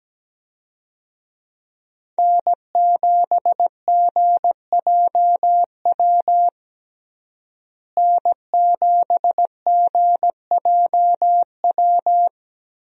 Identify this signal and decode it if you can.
Morse code, heard as a keyed tone, about 17 wpm, 700 Hz. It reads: N7GJW N7GJW
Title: USA Call Signs - 5 Characters - Repeated 2x - 17wpm